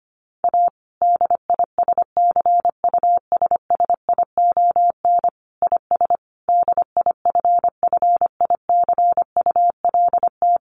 Morse code, heard as a keyed tone, about 25 wpm, 700 Hz. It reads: A DISCUSSION IS DIFFICULT
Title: Sentences from Top 1100 Words 25wpm